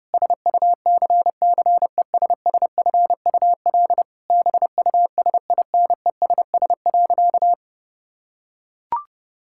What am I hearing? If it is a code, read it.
Morse code, heard as a keyed tone, about 30 wpm, 700 Hz. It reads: SUCCESSFUL BUSINESS.
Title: Sentences from Top 1000 Words - Review Part 6 - 30wpm